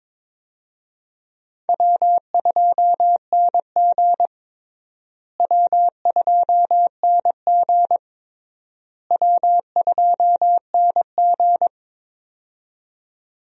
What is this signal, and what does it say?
Morse code, heard as a keyed tone, about 21 wpm, 700 Hz. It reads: W2NG W2NG W2NG
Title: USA Call Signs - 4 Characters - Repeated 3x - 22wpm